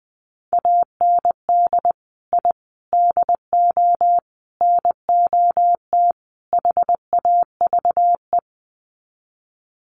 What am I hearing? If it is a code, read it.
Morse code, heard as a keyed tone, about 20 wpm, 700 Hz. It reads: AND I DO NOT HAVE
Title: Sets of 5 Words Encore 20wpm